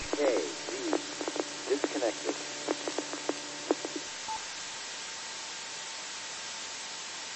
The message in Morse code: none

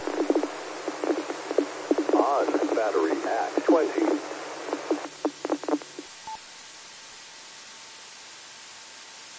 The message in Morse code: none

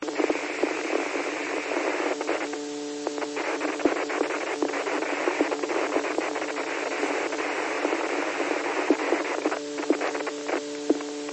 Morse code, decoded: none

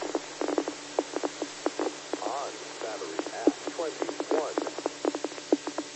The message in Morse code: none

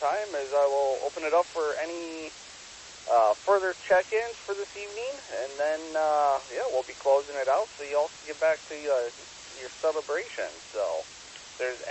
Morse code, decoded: none

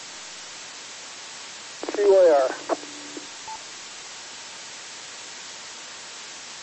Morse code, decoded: none

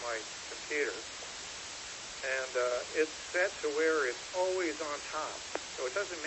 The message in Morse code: none

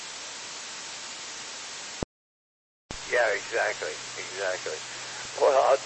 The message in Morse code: none

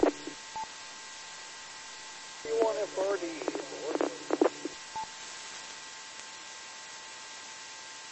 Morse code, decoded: none

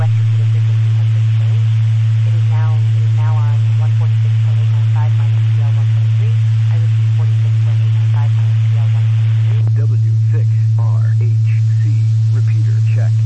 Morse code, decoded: none

